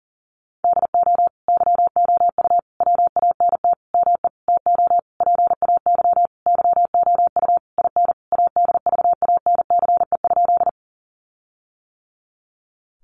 Morse code, heard as a keyed tone, about 40 wpm, 700 Hz. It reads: DO YOU WANT ME TO PAY YOU IN ADVANCE?
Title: Sentences from Top 1400 Words 40wpm